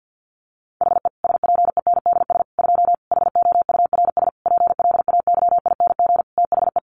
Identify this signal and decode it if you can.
Morse code, heard as a keyed tone, about 50 wpm, 700 Hz. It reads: HE SPENDS 2 HOURS PLAYING THE
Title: Additional Easy Phrases 50wpm